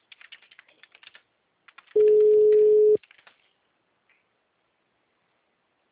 Ringback tone (Europe, 425 Hz).